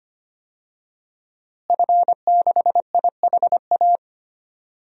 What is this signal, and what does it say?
Morse code, heard as a keyed tone, about 25 wpm, 700 Hz. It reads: F6IHA
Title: DXCC-100 Call Signs - 5 Characters - Repeated 3x - 25wpm